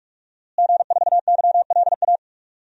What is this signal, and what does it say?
Morse code, heard as a keyed tone, about 46 wpm, 700 Hz. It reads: G4YLA
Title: DXCC-100 Call Signs - 5 Characters - Repeated 3x - 45wpm